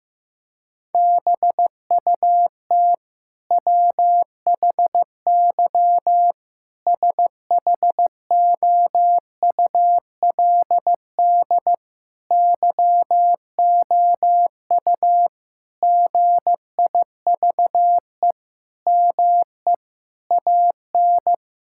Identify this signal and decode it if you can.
Morse code, heard as a keyed tone, about 15 wpm, 700 Hz. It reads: BUT WHY SHOULD YOU GIVE ME AN
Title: Sentences from Top 1000 Words - Review Part 6 - 15wpm